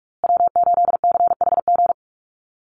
Morse code, heard as a keyed tone, about 44 wpm, 700 Hz. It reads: W8CHG
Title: USA Call Signs Repeated 3x 45wpm